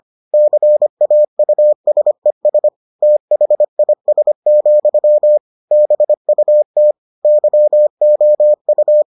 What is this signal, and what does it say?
Morse code, heard as a keyed tone, about 25 wpm, 600 Hz. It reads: CAUSES THIS, BUT YOU